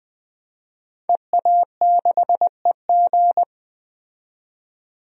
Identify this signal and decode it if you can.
Morse code, heard as a keyed tone, about 20 wpm, 700 Hz. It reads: EA6EG
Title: DXCC-100 Call Signs - 5 Characters - 20wpm